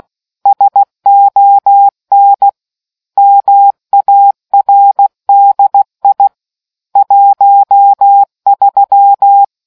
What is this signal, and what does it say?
Morse code, heard as a keyed tone, about 16 wpm, 800 Hz. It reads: SON MARDI 13